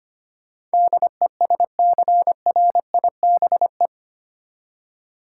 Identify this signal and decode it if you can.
Morse code, heard as a keyed tone, about 25 wpm, 700 Hz. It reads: DESCRIBE
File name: Eight Letter Words - 25wpm